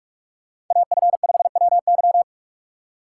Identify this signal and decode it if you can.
Morse code, heard as a keyed tone, about 43 wpm, 700 Hz. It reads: AF5WY